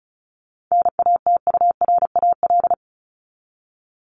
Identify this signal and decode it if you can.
Morse code, heard as a keyed tone, about 34 wpm, 700 Hz. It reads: NATURAL
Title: Top 501-750 Words 35wpm